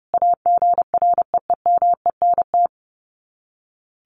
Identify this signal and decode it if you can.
Morse code, heard as a keyed tone, about 28 wpm, 700 Hz. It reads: AGREEMENT
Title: Sets of 5 Words Encore 30wpm